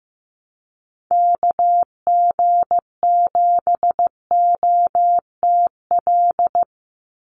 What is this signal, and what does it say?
Morse code, heard as a keyed tone, about 15 wpm, 700 Hz. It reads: KG7OTL
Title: USA Call Signs 15wpm